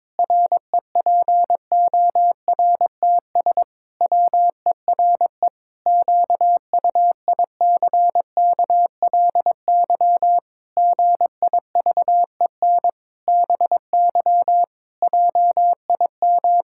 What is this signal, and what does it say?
Morse code, heard as a keyed tone, about 22 wpm, 700 Hz. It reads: REPORTS WERE QUICKLY GIVEN BY JIM
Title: Pangram Challenge v2 - 22wpm